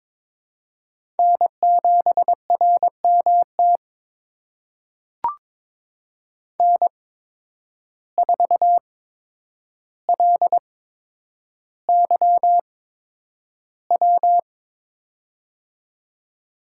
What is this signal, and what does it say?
Morse code, heard as a keyed tone, about 23 wpm, spaced as for 10 wpm, 700 Hz. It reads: N7RMT N4LYW
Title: USA Call Signs - 5 Characters - 8x Character Spacing - 22wpm